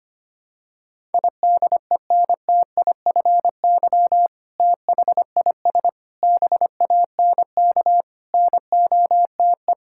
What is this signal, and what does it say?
Morse code, heard as a keyed tone, about 25 wpm, 700 Hz. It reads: IDENTIFY THIS BANK NOTE
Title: Sentences from Top 1000 Words - Review Part 6 - 25wpm